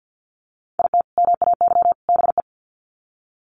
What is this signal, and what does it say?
Morse code, heard as a keyed tone, about 50 wpm, 700 Hz. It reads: IT MAY BE
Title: Sets of 3 Words 50wpm